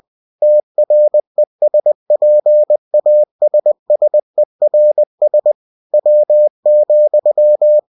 Morse code, heard as a keyed tone, about 20 wpm, 600 Hz. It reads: TRESPASSERS W,
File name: cw-books / winnie-the-pooh / winnie-20wpm-page-0038